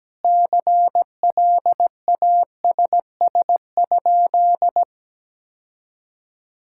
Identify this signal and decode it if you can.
Morse code, heard as a keyed tone, about 17 wpm, 700 Hz. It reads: CLASS?
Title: Sentences from Top 1000 Words - Review Part 6 - 17wpm